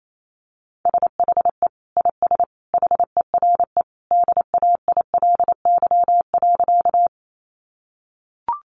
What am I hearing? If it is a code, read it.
Morse code, heard as a keyed tone, about 28 wpm, 700 Hz. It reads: SHE IS HERE DAILY.
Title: Sentences from Top 1100 Words 28wpm